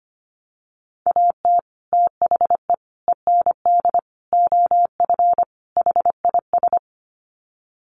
Morse code, heard as a keyed tone, about 25 wpm, 700 Hz. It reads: AT THE END OF HIS
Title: Sets of 5 Words Encore 25wpm